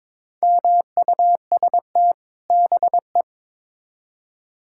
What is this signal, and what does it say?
Morse code, heard as a keyed tone, about 22 wpm, 700 Hz. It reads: MUST BE
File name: Build A Phrase - 2-Word Phrases - 22wpm